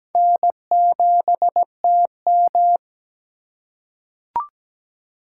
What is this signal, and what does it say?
Morse code, heard as a keyed tone, about 17 wpm, 700 Hz. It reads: N7TM
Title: USA Call Signs - 4 Characters - 2x Character Spacing - 17wpm